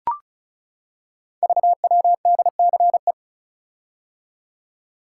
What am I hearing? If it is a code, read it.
Morse code, heard as a keyed tone, about 36 wpm, 700 Hz. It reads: VWDCE
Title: Code Groups - 5 Letters - 35wpm